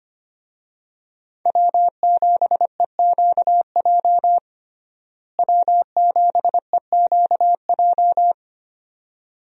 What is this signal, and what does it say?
Morse code, heard as a keyed tone, about 25 wpm, 700 Hz. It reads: W7EQJ W7EQJ
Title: USA Call Signs - 5 Characters - Repeated 2x - 25wpm